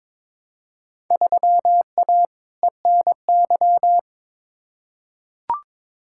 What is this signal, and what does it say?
Morse code, heard as a keyed tone, about 22 wpm, 700 Hz. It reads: 3A ENY
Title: Field Day Exchanges - 22wpm